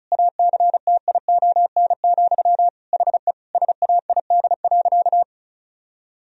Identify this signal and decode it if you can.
Morse code, heard as a keyed tone, about 34 wpm, 700 Hz. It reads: ACTION, HE SAID.
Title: Sentences from Top 1000 Words - Review Part 4 - 35wpm